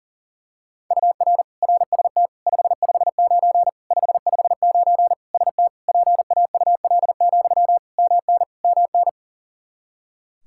Morse code, heard as a keyed tone, about 40 wpm, 700 Hz. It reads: UR RST 559 559 ST PAUL, MN MN
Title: Practice QSO - Exchange 1 - Beginning - 40wpm